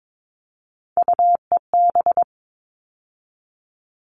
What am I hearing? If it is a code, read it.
Morse code, heard as a keyed tone, about 22 wpm, 700 Hz. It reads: UEB
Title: All Cut Numbers - 3-Digit 22wpm